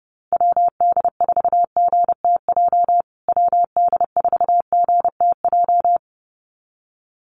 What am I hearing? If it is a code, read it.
Morse code, heard as a keyed tone, about 30 wpm, 700 Hz. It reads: WD4GTJ WD4GTJ
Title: USA Call Signs Repeated 2x 30wpm